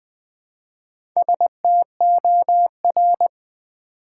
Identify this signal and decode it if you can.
Morse code, heard as a keyed tone, about 20 wpm, 700 Hz. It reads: STOR
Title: Build A Word - 1 Letter At A Time - 5-Letter Words - 20wpm